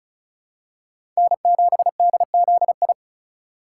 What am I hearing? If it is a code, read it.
Morse code, heard as a keyed tone, about 36 wpm, 700 Hz. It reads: N7DZI
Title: USA Call Signs - 2x Character Spacing - 35wpm